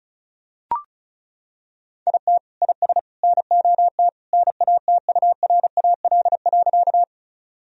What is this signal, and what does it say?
Morse code, heard as a keyed tone, about 34 wpm, 700 Hz. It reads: IT IS NOT NATURAL.
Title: Sentences from Top 1000 Words - Review Part 4 - 35wpm